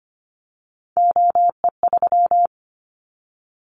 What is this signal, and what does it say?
Morse code, heard as a keyed tone, about 25 wpm, 700 Hz. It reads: OE3
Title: Callsign Prefixes for Top 100 DXCC - 25wpm